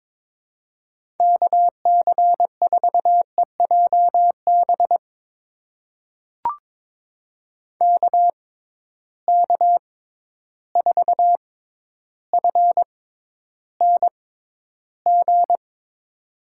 Morse code, heard as a keyed tone, about 22 wpm, spaced as for 11 wpm, 700 Hz. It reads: KC4EJB KK4FNG
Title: USA Call Signs - 6 Characters - 6x Character Spacing - 22wpm